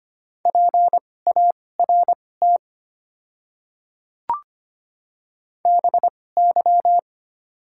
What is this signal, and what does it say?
Morse code, heard as a keyed tone, about 25 wpm, spaced as for 18 wpm, 700 Hz. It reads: PART BY
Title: 2x Character Spacing - Top 300 Words - 25wpm